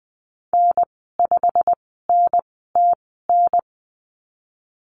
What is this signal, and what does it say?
Morse code, heard as a keyed tone, about 20 wpm, spaced as for 15 wpm, 700 Hz. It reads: N5NTN
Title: 088 Lesson 15g - Number 5 - Call Signs 20wpm